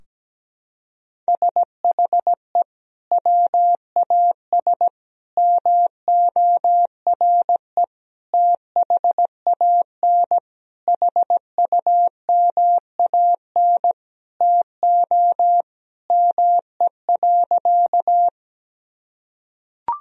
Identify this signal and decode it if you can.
Morse code, heard as a keyed tone, about 17 wpm, 700 Hz. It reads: SHE WAS MORE THAN HUMAN TO ME.
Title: Sentences from Top 1000 Words - Review Part 3 - 17wpm